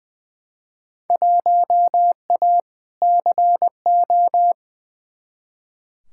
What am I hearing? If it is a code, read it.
Morse code, heard as a keyed tone, about 21 wpm, 700 Hz. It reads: 1A CO
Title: Field Day Exchanges - 20wpm